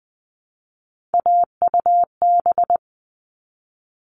Morse code, heard as a keyed tone, about 20 wpm, 700 Hz. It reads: AUB
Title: All Cut Numbers - 3-Digit 20wpm